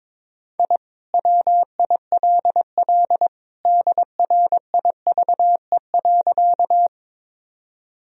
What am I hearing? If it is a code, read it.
Morse code, heard as a keyed tone, about 22 wpm, 700 Hz. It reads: I WILL DRIVE.
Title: Sentences from Top 1000 Words - Review Part 2 - 22wpm